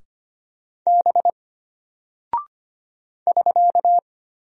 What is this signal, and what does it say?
Morse code, heard as a keyed tone, about 25 wpm, 700 Hz. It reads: B <SK>
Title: Single Character (Letter:Number:Punctuation:Prosign) Alt 25wpm